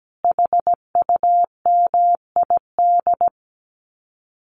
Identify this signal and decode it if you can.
Morse code, heard as a keyed tone, about 17 wpm, 700 Hz. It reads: HUMID
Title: Common QSO Phrases Encore 17wpm